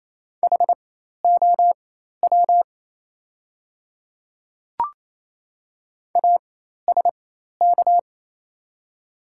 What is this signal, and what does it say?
Morse code, heard as a keyed tone, about 28 wpm, spaced as for 13 wpm, 700 Hz. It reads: HOW ASK
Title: 4x Character Spacing - Top 300 Words - 28wpm